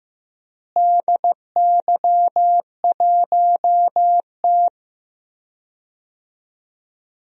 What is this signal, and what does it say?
Morse code, heard as a keyed tone, about 15 wpm, 700 Hz. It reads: DY1T
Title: DXCC-100 Call Signs - 4 Characters - Repeated 3x - 15wpm